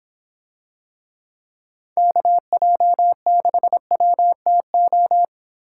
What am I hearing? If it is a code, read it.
Morse code, heard as a keyed tone, about 24 wpm, 700 Hz. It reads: KJ6WTO